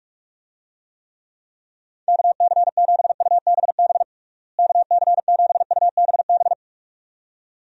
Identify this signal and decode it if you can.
Morse code, heard as a keyed tone, about 46 wpm, 700 Hz. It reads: KC7UBB KC7UBB
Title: USA Call Signs - 6 Characters - Repeated 2x - 45wpm